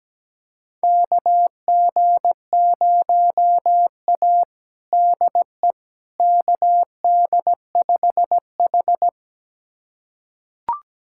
Morse code, heard as a keyed tone, about 17 wpm, 700 Hz. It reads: KG0A DE KD5H
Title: USA Callsign DE Callsign - 4 Characters - 17wpm